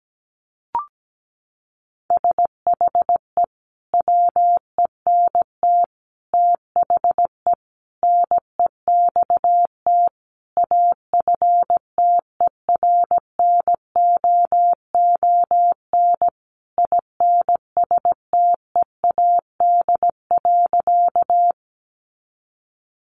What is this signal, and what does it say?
Morse code, heard as a keyed tone, about 17 wpm, 700 Hz. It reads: SHE WENT THE NEXT AFTERNOON INSTEAD.
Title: Sentences from Top 1500 Words 17wpm